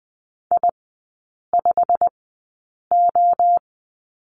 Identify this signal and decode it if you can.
Morse code, heard as a keyed tone, about 20 wpm, spaced as for 8 wpm, 700 Hz. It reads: I5O
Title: 138 Lesson 23e - Letter f - Three Characters 20wpm